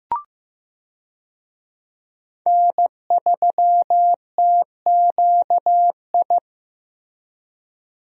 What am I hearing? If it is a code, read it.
Morse code, heard as a keyed tone, about 15 wpm, 700 Hz. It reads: N3TQI
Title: USA Call Signs - 5 Characters - Repeated 2x - 15wpm